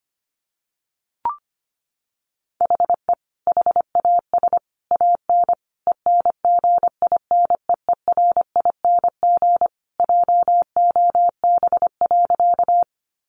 Morse code, heard as a keyed tone, about 25 wpm, 700 Hz. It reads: HE HAS AN ENGINEERING JOB.
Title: Sentences from Top 1500 Words 25wpm